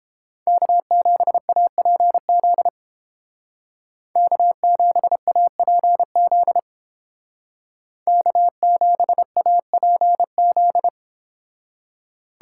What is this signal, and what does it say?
Morse code, heard as a keyed tone, about 29 wpm, 700 Hz. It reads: K7APZ K7APZ K7APZ